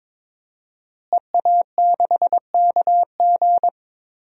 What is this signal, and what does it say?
Morse code, heard as a keyed tone, about 22 wpm, 700 Hz. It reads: EA6KG
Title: DXCC-100 Call Signs - 5 Characters - Repeated 2x - 22wpm